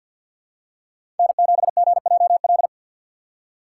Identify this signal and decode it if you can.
Morse code, heard as a keyed tone, about 49 wpm, 700 Hz. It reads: N7GJL
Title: USA Call Signs - 5 Characters - 6x Character Spacing - 50wpm